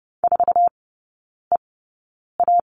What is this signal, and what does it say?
Morse code, heard as a keyed tone, about 30 wpm, spaced as for 9 wpm, 700 Hz. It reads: 4EA
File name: course / 30:10wpm / 087 Lesson 15e - Number 5 - Three Characters 30wpm